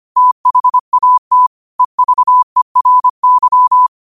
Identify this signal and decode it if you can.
Morse code, heard as a keyed tone, about 25 wpm, 1000 Hz. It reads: THAT EVERY
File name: qotdfiles / 25wpm